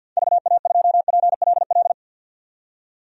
Morse code, heard as a keyed tone, about 51 wpm, 700 Hz. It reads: VA2PLL
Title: DXCC-100 Call Signs - 6 Characters - 50wpm